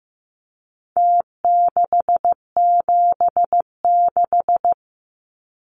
Morse code, heard as a keyed tone, about 15 wpm, 700 Hz. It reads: T676
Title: Numbers - Contest Style - 4-Digit 15wpm